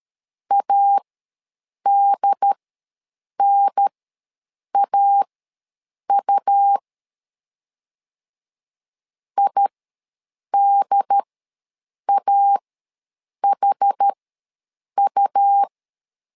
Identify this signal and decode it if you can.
Morse code, heard as a keyed tone, about 13 wpm, spaced as for 7 wpm, 800 Hz. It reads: ADNAU IDAHU
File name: LEZIONI / 4 Q ws 12wpm 1912